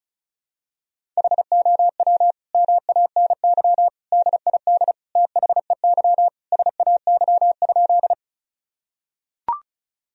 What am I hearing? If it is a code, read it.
Morse code, heard as a keyed tone, about 34 wpm, 700 Hz. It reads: HOW MANY DID THEY SAY?
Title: Sentences from Top 1000 Words - Review Part 1 - 35wpm